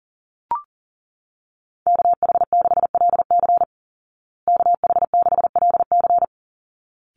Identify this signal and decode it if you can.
Morse code, heard as a keyed tone, about 40 wpm, 700 Hz. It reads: KH6LC KH6LC
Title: Call Signs Repeated 3x - 40wpm